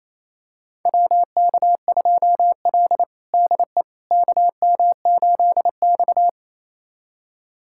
Morse code, heard as a keyed tone, about 28 wpm, 700 Hz. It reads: WK2L DE KM8X
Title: USA Callsign DE Callsign - 4 Characters - 28wpm